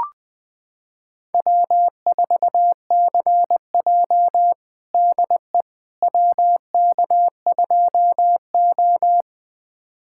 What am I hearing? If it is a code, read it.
Morse code, heard as a keyed tone, about 20 wpm, 700 Hz. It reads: W4CJ DE WK2O
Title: USA Callsign DE Callsign - 4 Characters - 20wpm